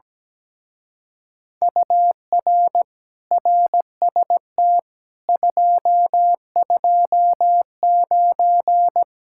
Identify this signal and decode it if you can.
Morse code, heard as a keyed tone, about 17 wpm, 700 Hz. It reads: UR RST 229
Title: Speed-Racing Common QSO Phrases - 17wpm